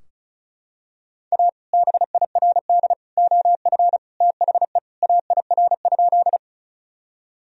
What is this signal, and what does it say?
Morse code, heard as a keyed tone, about 36 wpm, 700 Hz. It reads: A BIRD OF THE AIR?
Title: Sentences from Top 1000 Words - Review Part 2 - 35wpm